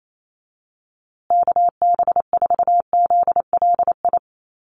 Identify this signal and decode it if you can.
Morse code, heard as a keyed tone, about 28 wpm, 700 Hz. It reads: KB4ZLI
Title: USA Call Signs Repeated 3x 28wpm